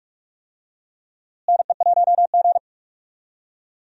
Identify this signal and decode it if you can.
Morse code, heard as a keyed tone, about 46 wpm, 700 Hz. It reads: NE1G
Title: USA Call Signs - 4 Characters - Repeated 2x - 45wpm